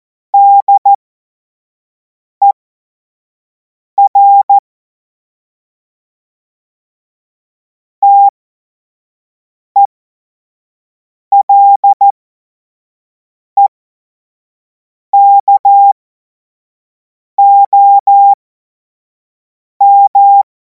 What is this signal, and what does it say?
Morse code, heard as a keyed tone, about 14 wpm, spaced as for 5 wpm, 800 Hz. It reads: DER TELEKOM
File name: Morsekurs / Witze / Witz 10. März 2026 25 F70